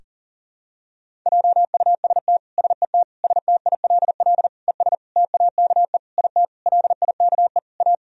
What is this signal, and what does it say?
Morse code, heard as a keyed tone, about 40 wpm, 700 Hz. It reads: JUST SET STILL ES TAKE IT LIKE A
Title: Sentences from Top 300 Words and Abbreviations 40wpm